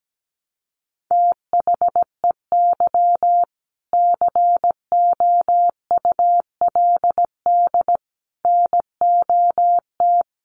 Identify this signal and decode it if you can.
Morse code, heard as a keyed tone, about 17 wpm, 700 Hz. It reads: THEY COULD NOT
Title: Sentences from Top 1100 Words 17wpm